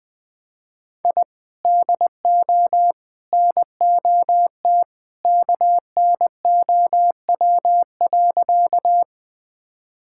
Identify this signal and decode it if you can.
Morse code, heard as a keyed tone, about 21 wpm, 700 Hz. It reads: I DO NOT KNOW.
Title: Sentences with 3-Word Phrases - 20wpm